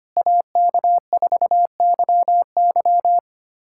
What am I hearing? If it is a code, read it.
Morse code, heard as a keyed tone, about 25 wpm, 700 Hz. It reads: AK4YY
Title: USA Callsign DE Callsign - 5 Characters - 25wpm